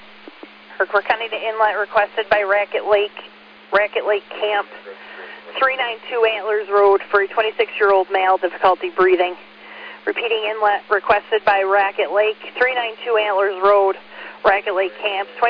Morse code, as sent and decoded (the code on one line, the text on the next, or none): none